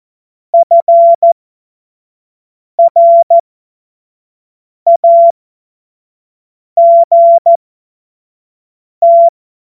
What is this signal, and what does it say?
Morse code, heard as a keyed tone, about 14 wpm, spaced as for 5 wpm, 680 Hz. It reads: FRAGT
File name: Morsekurs / Witze / Witz 14. September 2021, 25 F70